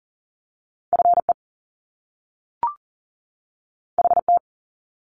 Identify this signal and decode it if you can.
Morse code, heard as a keyed tone, about 38 wpm, 700 Hz. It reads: FE HT
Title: Top Two Letter Combinations in Words 40wpm